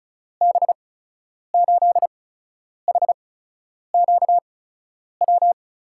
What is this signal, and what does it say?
Morse code, heard as a keyed tone, about 36 wpm, spaced as for 10 wpm, 700 Hz. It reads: B8HQW
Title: USA Call Signs - 6 Characters - 8x Character Spacing - 35wpm